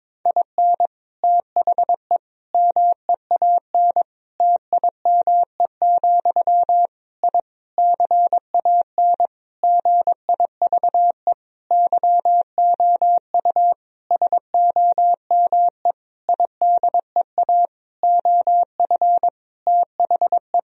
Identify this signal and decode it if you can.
Morse code, heard as a keyed tone, about 22 wpm, 700 Hz. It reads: IN THE MEAN TIME, I CAN GIVE YOU SOME IDEA OF THE
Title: Sentences from Top 1000 Words - Review Part 3 - 22wpm